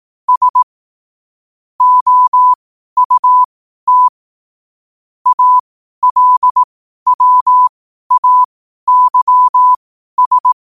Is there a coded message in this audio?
Yes: Morse code from 0.3 to 10.5 s, heard as a keyed tone, about 18 wpm, spaced as for 12 wpm, 1000 Hz. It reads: S OUT ALWAYS